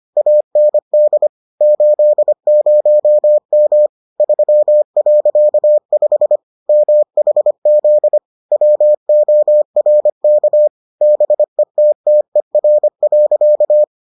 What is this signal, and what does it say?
Morse code, heard as a keyed tone, about 25 wpm, 600 Hz. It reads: AND 80M 3.5 MHZ WORK BETTER.